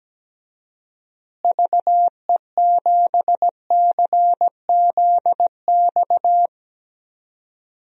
Morse code, heard as a keyed tone, about 17 wpm, 700 Hz. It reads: VE7CZX
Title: DXCC-100 Call Signs - 6 Characters - Repeated 2x - 17wpm